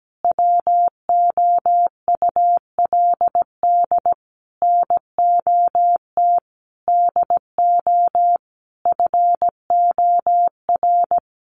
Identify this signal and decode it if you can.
Morse code, heard as a keyed tone, about 17 wpm, 700 Hz. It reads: WOULD NOT DO FOR
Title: Sentences from Top 600 Words 17wpm